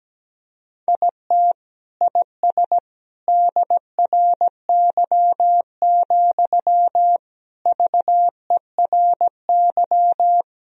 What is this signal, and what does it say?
Morse code, heard as a keyed tone, about 17 wpm, 700 Hz. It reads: IT IS DRY, VERY
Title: Sentences from Top 1000 Words - Review Part 3 - 17wpm